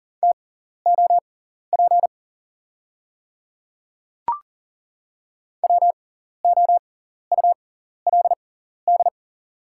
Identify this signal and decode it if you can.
Morse code, heard as a keyed tone, about 38 wpm, spaced as for 14 wpm, 700 Hz. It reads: TOP WOULD
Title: 6x Character Spacing - Top 500 Words - 40wpm